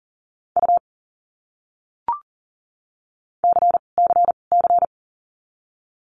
Morse code, heard as a keyed tone, about 40 wpm, 700 Hz. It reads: U C C C
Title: Abbreviations Repeated 3x - 40wpm